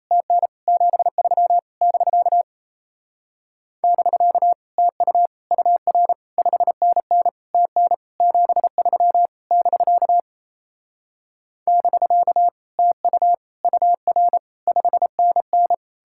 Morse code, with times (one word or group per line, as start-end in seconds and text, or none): 0.1-0.5 TN
0.7-1.6 73
1.8-2.4 <BK>
3.8-4.5 <BK>
4.8-5.3 TU
5.5-6.1 UR
6.4-7.3 5NN
7.5-7.9 TN
8.2-9.3 73
9.5-10.2 <BK>
11.7-12.5 <BK>
12.8-13.3 TU
13.6-14.4 UR
14.7-15.7 5NN